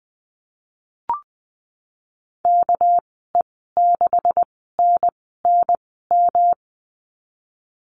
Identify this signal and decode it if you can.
Morse code, heard as a keyed tone, about 20 wpm, spaced as for 15 wpm, 700 Hz. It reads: KE6NNM
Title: 200 Lesson 32g - Letter k - Call Signs 20wpm